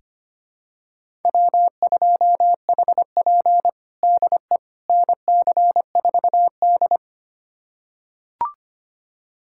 Morse code, heard as a keyed tone, about 25 wpm, 700 Hz. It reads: W2HP DE NC4D
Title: USA Callsign DE Callsign - 4 Characters - 25wpm